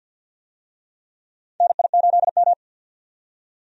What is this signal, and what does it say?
Morse code, heard as a keyed tone, about 53 wpm, 700 Hz. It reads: NI8M
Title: USA Call Signs - 4 Characters - 50wpm